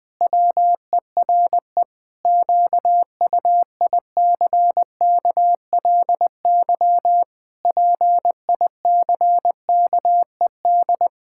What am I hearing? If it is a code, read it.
Morse code, heard as a keyed tone, about 21 wpm, 700 Hz. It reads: WERE QUICKLY PICKED
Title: Pangram Challenge v2 - 20wpm